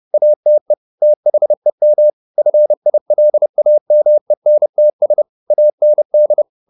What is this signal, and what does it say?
Morse code, heard as a keyed tone, about 30 wpm, 600 Hz. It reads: ATE THEM FILAMENTS AND
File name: the old man and the sea / the old man and the sea 1 30wpm